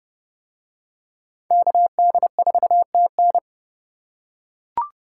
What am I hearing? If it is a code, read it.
Morse code, heard as a keyed tone, about 27 wpm, 700 Hz. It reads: KD4TN